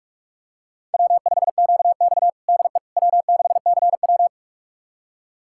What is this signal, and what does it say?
Morse code, heard as a keyed tone, about 44 wpm, 690 Hz. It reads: W5QX DE W6CW
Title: USA Callsign DE Callsign - 4 Characters - 45wpm